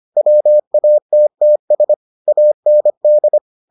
Morse code, heard as a keyed tone, about 25 wpm, 600 Hz. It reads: WATTS AND